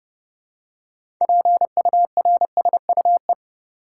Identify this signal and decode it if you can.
Morse code, heard as a keyed tone, about 30 wpm, 700 Hz. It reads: PURSUE
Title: Six Letter Words - 30wpm